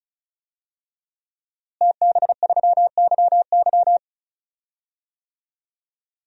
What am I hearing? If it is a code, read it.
Morse code, heard as a keyed tone, about 34 wpm, 700 Hz. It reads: TB3YY
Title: DXCC-100 Call Signs - 5 Characters - Repeated 2x - 35wpm